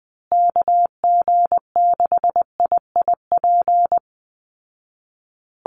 Morse code, heard as a keyed tone, about 20 wpm, 700 Hz. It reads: KG6IIP
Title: Call Signs Repeated 2x - 20wpm